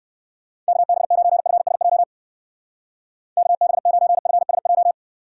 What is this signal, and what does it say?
Morse code, heard as a keyed tone, about 55 wpm, 700 Hz. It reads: KB9FSJ KB9FSJ
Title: Speed Racing - USA 2x3 Call Signs 45wpm